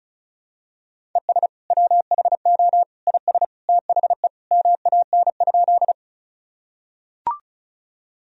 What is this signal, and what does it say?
Morse code, heard as a keyed tone, about 36 wpm, 700 Hz. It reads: ES WHO IS THE MAN?
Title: Sentences from Top 200 Words and Abbreviations 35wpm